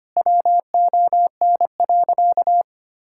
Morse code, heard as a keyed tone, about 25 wpm, 700 Hz. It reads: WON.
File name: practice / 25wpm / Sentences with 9-Word Phrases - 25wpm